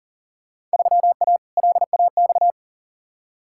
Morse code, heard as a keyed tone, about 40 wpm, 700 Hz. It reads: 3A LAX
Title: Field Day Exchanges - 40wpm